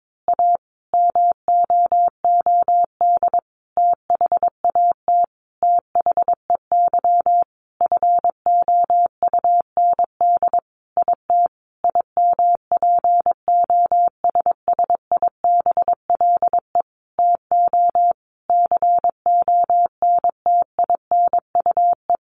Morse code, heard as a keyed tone, about 22 wpm, 700 Hz. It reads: A MOOD THAT THEY FOUND IT IMPOSSIBLE TO CONTINUE
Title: Sentences from Top 1100 Words 22wpm